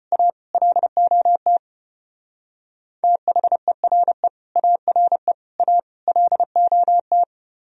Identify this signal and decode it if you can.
Morse code, heard as a keyed tone, about 30 wpm, 700 Hz. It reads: A LOT THERE ARE A LOT
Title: Speed-Racing - Sets of 4 Words 25wpm